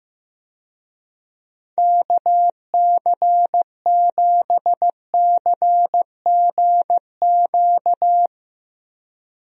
Morse code, heard as a keyed tone, about 16 wpm, 700 Hz. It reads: KC7CGQ
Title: USA Call Signs - 6 Characters - Repeated 2x - 15wpm